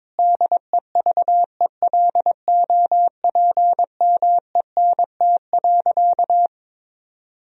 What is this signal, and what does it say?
Morse code, heard as a keyed tone, about 22 wpm, 700 Hz. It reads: DEVELOPMENT.
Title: Sentences from Top 1000 Words - Review Part 4 - 22wpm